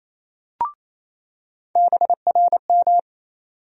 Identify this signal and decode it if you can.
Morse code, heard as a keyed tone, about 28 wpm, 700 Hz. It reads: BRM
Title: USA Call Sign Postfixes - 3 Letters 28wpm